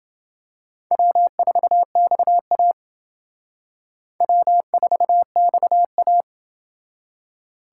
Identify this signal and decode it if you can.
Morse code, heard as a keyed tone, about 28 wpm, 690 Hz. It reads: W4XA W4XA